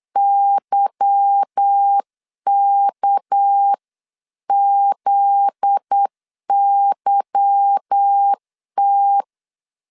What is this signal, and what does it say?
Morse code, heard as a keyed tone, about 8 wpm, 800 Hz. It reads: YKZYT
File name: CW / DET / 1 9 1500C S B 8wpm 5624